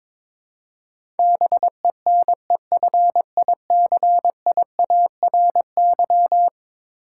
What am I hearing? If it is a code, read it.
Morse code, heard as a keyed tone, about 23 wpm, 700 Hz. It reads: BENEFICIARY
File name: practice / 22wpm / Eleven Letter Words - 22wpm